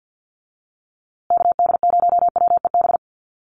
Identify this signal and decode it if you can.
Morse code, heard as a keyed tone, about 50 wpm, 700 Hz. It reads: KD0WEB
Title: USA Call Signs Repeated 2x 50wpm